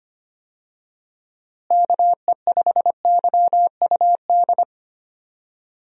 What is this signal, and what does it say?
Morse code, heard as a keyed tone, about 25 wpm, 700 Hz. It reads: KE5YUD
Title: USA Call Signs - 6 Characters - Repeated 2x - 25wpm